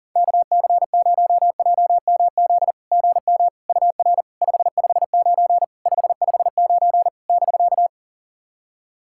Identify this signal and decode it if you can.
Morse code, heard as a keyed tone, about 40 wpm, 690 Hz. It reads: KC0JMZ GM UR 559 559 <BK>
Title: SOTA QSO Practice 40wpm